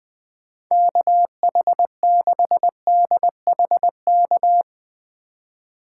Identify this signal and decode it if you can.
Morse code, heard as a keyed tone, about 20 wpm, 700 Hz. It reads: KH6DHK
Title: K1USN SST QSO Practice - 20wpm